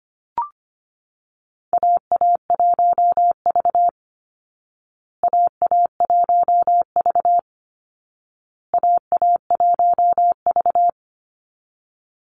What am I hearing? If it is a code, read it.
Morse code, heard as a keyed tone, about 25 wpm, 700 Hz. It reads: AA1V AA1V AA1V
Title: Call Signs Repeated 2x - 25wpm